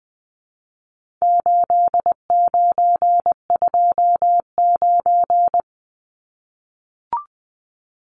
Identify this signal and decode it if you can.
Morse code, heard as a keyed tone, about 19 wpm, 700 Hz. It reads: 8929
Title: Numbers - 4-Digit 20wpm